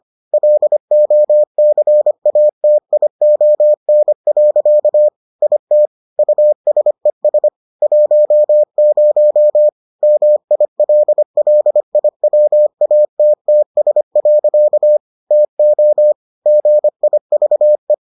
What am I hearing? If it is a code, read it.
Morse code, heard as a keyed tone, about 25 wpm, 600 Hz. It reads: LOCATION. IT USES 10 MILLIWATTS. TO GIVE